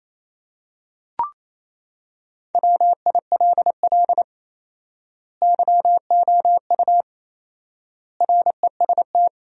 Morse code, heard as a keyed tone, about 28 wpm, 700 Hz. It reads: WILL YOU REST
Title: 4x Word-Spacing - Sentences from Top 500 Words - 28wpm